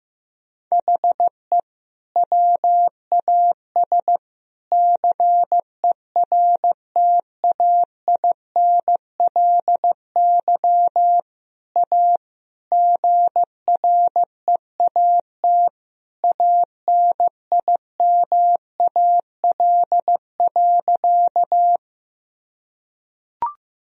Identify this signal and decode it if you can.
Morse code, heard as a keyed tone, about 16 wpm, 700 Hz. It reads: HE WAS CERTAINLY A GREAT ANIMAL.
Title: Sentences from Top 1000 Words - Review Part 1 - 15wpm